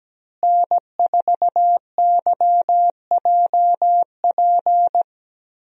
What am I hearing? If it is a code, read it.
Morse code, heard as a keyed tone, about 17 wpm, 700 Hz. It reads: N4YJP
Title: USA Call Signs - 5 Characters - Repeated 3x - 17wpm